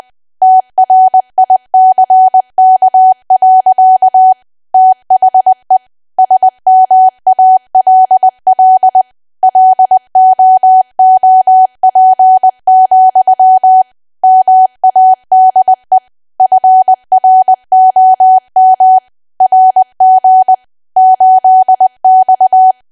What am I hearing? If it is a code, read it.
Morse code, heard as a keyed tone, about 20 wpm, 750 Hz. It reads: TRICK. THE SMALL LOOP, MADE FROM RG 8X
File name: cwa / W1AW-Level3 / 20WPM